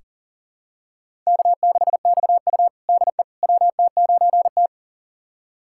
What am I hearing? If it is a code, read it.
Morse code, heard as a keyed tone, about 37 wpm, 700 Hz. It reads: K6XU DE WT9T
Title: USA Callsign DE Callsign - 4 Characters - 40wpm